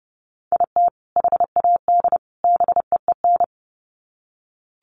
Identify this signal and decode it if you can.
Morse code, heard as a keyed tone, about 30 wpm, 700 Hz. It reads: IT HAD BEEN
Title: Sets of 3 Words 30wpm